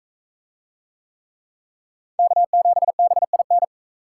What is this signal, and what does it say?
Morse code, heard as a keyed tone, about 40 wpm, 700 Hz. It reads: K7BIN